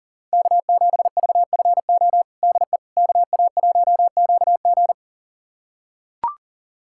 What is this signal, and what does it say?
Morse code, heard as a keyed tone, about 40 wpm, 700 Hz. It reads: K7VFO DE KA1QG
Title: USA Callsign DE Callsign - 5 Characters - 40wpm